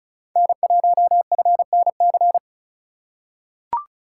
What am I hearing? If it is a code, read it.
Morse code, heard as a keyed tone, about 36 wpm, 700 Hz. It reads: N1FNC
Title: USA Call Signs - 5 Characters - 2x Character Spacing - 35wpm